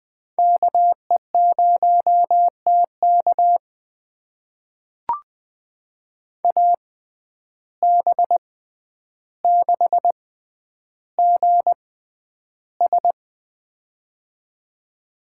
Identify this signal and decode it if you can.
Morse code, heard as a keyed tone, about 20 wpm, spaced as for 10 wpm, 700 Hz. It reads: KE0TK AB6GS
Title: USA Call Signs - 5 Characters - 6x Character Spacing - 20wpm